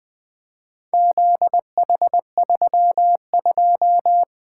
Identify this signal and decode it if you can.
Morse code, heard as a keyed tone, about 20 wpm, 700 Hz. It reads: ZH32
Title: Four Letter-Number Combinations - 20wpm